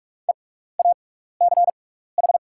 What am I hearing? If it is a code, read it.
Morse code, heard as a keyed tone, about 44 wpm, spaced as for 15 wpm, 690 Hz. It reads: EACH
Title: 6x Character Spacing - Top 100 Words - 45wpm